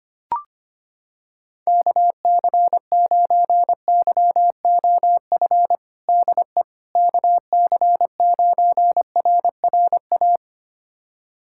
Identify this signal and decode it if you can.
Morse code, heard as a keyed tone, about 25 wpm, 700 Hz. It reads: KC9YOF DE KC9RRA
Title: USA Callsign DE Callsign 25wpm